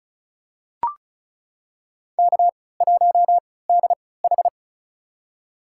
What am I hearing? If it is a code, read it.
Morse code, heard as a keyed tone, about 36 wpm, spaced as for 20 wpm, 700 Hz. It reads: K1DH
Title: USA Call Signs - 4 Characters - 3x Character Spacing - 35wpm